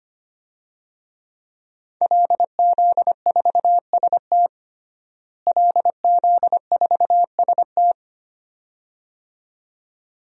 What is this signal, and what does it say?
Morse code, heard as a keyed tone, about 25 wpm, 700 Hz. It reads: LZ4ST LZ4ST
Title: DXCC-100 Call Signs - 5 Characters - Repeated 2x - 25wpm